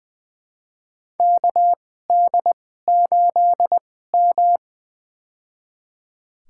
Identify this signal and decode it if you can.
Morse code, heard as a keyed tone, about 20 wpm, spaced as for 15 wpm, 700 Hz. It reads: KD8M
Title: USA Call Signs - 4 Characters - 2x Character Spacing - 20wpm